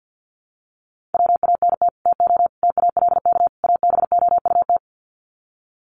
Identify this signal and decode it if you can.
Morse code, heard as a keyed tone, about 50 wpm, 700 Hz. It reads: WANT TO TALK ABOUT
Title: Sets of 4 Words 50wpm